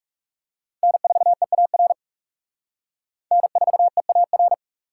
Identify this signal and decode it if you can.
Morse code, heard as a keyed tone, about 35 wpm, 700 Hz. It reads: N4EAR N4EAR
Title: Speed Racing - USA 1x3 Call Signs 30wpm